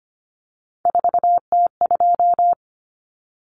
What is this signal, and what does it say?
Morse code, heard as a keyed tone, about 25 wpm, 700 Hz. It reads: 4T2
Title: Common Cut Numbers - 3-Digit 25wpm